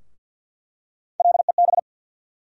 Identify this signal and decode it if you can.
Morse code, heard as a keyed tone, about 50 wpm, 700 Hz. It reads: LEB
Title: USA Call Sign Postfixes - 3 Letters 50wpm